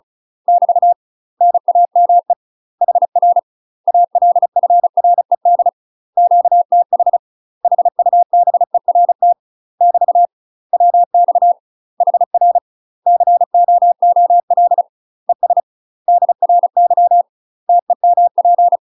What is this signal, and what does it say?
Morse code, heard as a keyed tone, about 36 wpm, 700 Hz. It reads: <BT> NAME HR ALFRED QTH HUBERT <BT> WX HR COOL ES DRY TEMP
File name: UK MASTER QSOs TRAINING / 35WPM / UK MASTER QSOs 0009